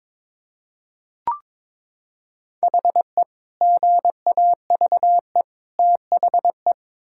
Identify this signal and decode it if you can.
Morse code, heard as a keyed tone, about 21 wpm, 700 Hz. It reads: HE GAVE THE
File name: practice / 22wpm / Sentences with 4-Word Phrases - 22wpm